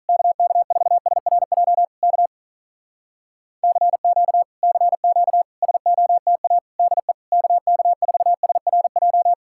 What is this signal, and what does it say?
Morse code, heard as a keyed tone, about 42 wpm, 700 Hz. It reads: KK4SRJ K CQ CQ SOTA DE KK4SRJ
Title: Speed-Racing Common QSO Phrases Encore - 35wpm